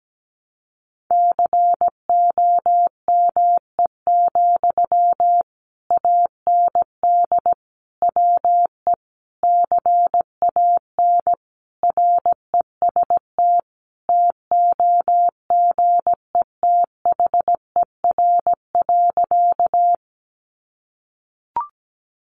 Morse code, heard as a keyed tone, about 17 wpm, 700 Hz. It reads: COME, AND WE CAN REST TOGETHER.
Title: Sentences from Top 500 Words 17wpm